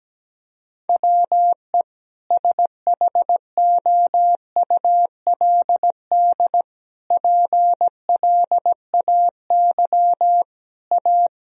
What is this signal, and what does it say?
Morse code, heard as a keyed tone, about 17 wpm, 700 Hz. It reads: WE SHOULD PLAY A
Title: Sentences from Top 1000 Words - Review Part 3 - 17wpm